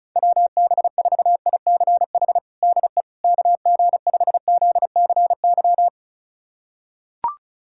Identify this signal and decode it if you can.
Morse code, heard as a keyed tone, about 36 wpm, 700 Hz. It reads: WB4ICH DE KG5ZCY
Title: USA Callsign DE Callsign - 6 Characters - 35wpm